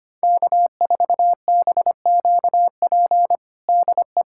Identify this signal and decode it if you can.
Morse code, heard as a keyed tone, about 25 wpm, 700 Hz. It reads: K4BQP DE
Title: USA Callsign DE Callsign 25wpm